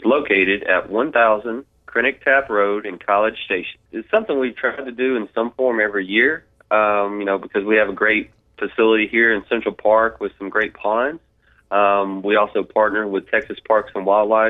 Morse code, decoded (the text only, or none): none